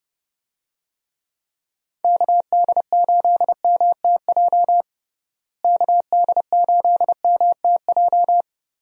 Morse code, heard as a keyed tone, about 30 wpm, 700 Hz. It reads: KD8MTJ KD8MTJ
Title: USA Call Signs - 6 Characters - Repeated 3x - 30wpm